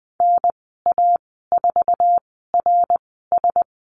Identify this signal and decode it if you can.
Morse code, heard as a keyed tone, about 20 wpm, spaced as for 15 wpm, 700 Hz. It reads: NA4RS
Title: 088 Lesson 15g - Number 5 - Call Signs 20wpm